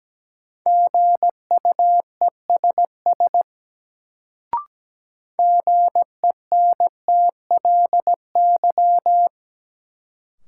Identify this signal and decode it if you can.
Morse code, heard as a keyed tone, about 17 wpm, 700 Hz. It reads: GUESS GENTLY
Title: Sentences from Top 1500 Words - New Word Drill - 17wpm